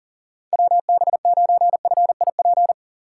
Morse code, heard as a keyed tone, about 40 wpm, 700 Hz. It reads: WB9FIP
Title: K1USN SST QSO Practice - 40wpm